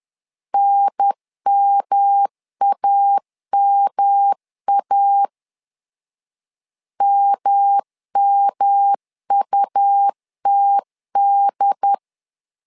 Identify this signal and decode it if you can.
Morse code, heard as a keyed tone, about 11 wpm, 800 Hz. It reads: NMAMA MMUTD
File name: LEZIONI / 3 MWRO / 3 M 10wpm 1828